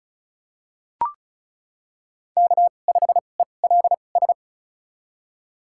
Morse code, heard as a keyed tone, about 36 wpm, spaced as for 26 wpm, 700 Hz. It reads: K5ELS
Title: USA Call Signs - 5 Characters - 2x Character Spacing - 35wpm